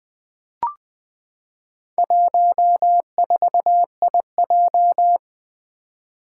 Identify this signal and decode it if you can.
Morse code, heard as a keyed tone, about 20 wpm, 700 Hz. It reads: 14IJ
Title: Four Challenging Letter-Number Combinations 20wpm